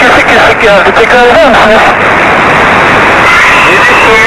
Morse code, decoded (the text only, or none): none